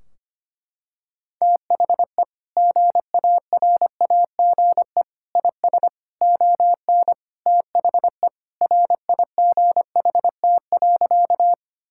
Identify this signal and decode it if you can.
Morse code, heard as a keyed tone, about 25 wpm, 700 Hz. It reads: THE GARAGE IS ON THE RIGHT.
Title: Sentences from Top 1600 Words 25wpm